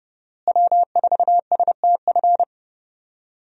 Morse code, heard as a keyed tone, about 30 wpm, 700 Hz. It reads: W4STF
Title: USA Call Signs - 5 Characters - Repeated 2x - 30wpm